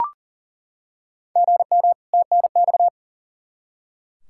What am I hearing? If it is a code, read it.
Morse code, heard as a keyed tone, about 38 wpm, 700 Hz. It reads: GM TNX
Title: Practice QSO - Exchange 1 - Beginning - 40wpm